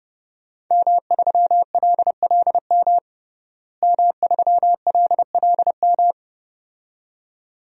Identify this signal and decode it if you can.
Morse code, heard as a keyed tone, about 32 wpm, 700 Hz. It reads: M3LLM M3LLM
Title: DXCC-100 Call Signs - 5 Characters - Repeated 2x - 30wpm